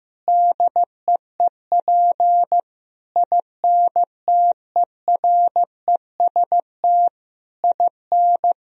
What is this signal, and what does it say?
Morse code, heard as a keyed tone, about 16 wpm, 700 Hz. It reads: DEEP INTEREST IN
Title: Sentences from Top 1000 Words - Review Part 3 - 15wpm